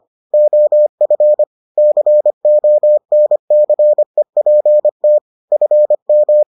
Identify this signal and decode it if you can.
Morse code, heard as a keyed tone, about 25 wpm, 600 Hz. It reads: OF CONCEPT FM